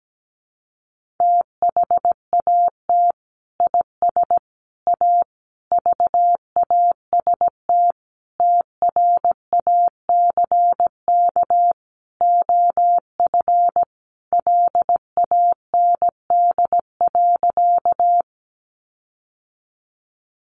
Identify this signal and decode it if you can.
Morse code, heard as a keyed tone, about 17 wpm, 700 Hz. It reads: THAT IS A VAST TRACK OF LAND.
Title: Sentences from Top 1300 Words 17wpm